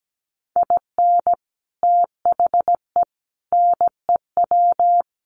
Morse code, heard as a keyed tone, about 17 wpm, 700 Hz. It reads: IN THE NEW
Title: Sets of 4 Words 17wpm